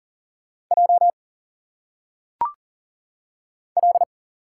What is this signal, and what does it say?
Morse code, heard as a keyed tone, about 39 wpm, 700 Hz. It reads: J L
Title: Single Letters - Phonetic Pronunciation 40wpm